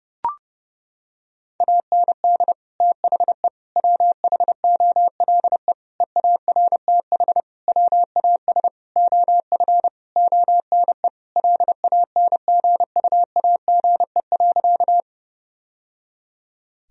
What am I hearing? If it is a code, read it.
Morse code, heard as a keyed tone, about 32 wpm, 700 Hz. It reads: AND THE WHOLE EARTH WAS OF ONE LANGUAGE.
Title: Sentences from Top 1000 Words - Review Part 3 - 30wpm